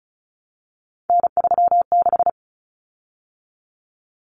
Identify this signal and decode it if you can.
Morse code, heard as a keyed tone, about 34 wpm, 700 Hz. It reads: N36
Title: Common Cut Numbers - 3-Digit 35wpm